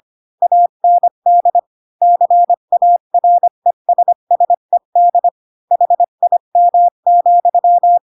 Morse code, heard as a keyed tone, about 25 wpm, 700 Hz. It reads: AND CARESSED HIM,
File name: Pinocchio / Pinocchio ch7 20 25 ik1hge